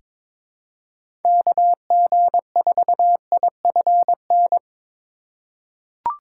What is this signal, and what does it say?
Morse code, heard as a keyed tone, about 21 wpm, 700 Hz. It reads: KG4IFN
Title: USA Call Signs - 3x Character Spacing - 22wpm